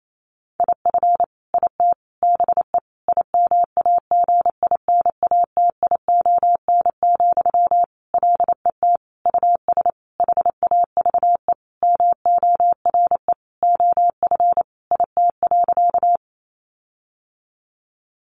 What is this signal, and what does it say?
Morse code, heard as a keyed tone, about 28 wpm, 700 Hz. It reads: IF IT BE IMAGINATION, LET US HAVE MORE OF IT.
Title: Sentences from Top 1300 Words 28wpm